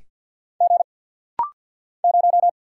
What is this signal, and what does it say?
Morse code, heard as a keyed tone, about 48 wpm, 700 Hz. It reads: G 0
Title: Single Letter-Number - Rapid-Fire 50wpm